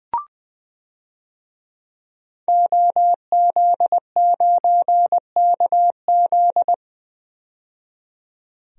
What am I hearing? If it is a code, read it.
Morse code, heard as a keyed tone, about 20 wpm, 700 Hz. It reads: OZ9KZ
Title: DXCC-100 Call Signs - 5 Characters - 20wpm